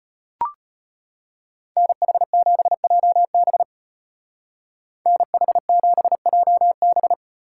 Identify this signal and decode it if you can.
Morse code, heard as a keyed tone, about 30 wpm, 700 Hz. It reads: NH7JB NH7JB
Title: Speed Racing - USA 2x2 Call Signs 25wpm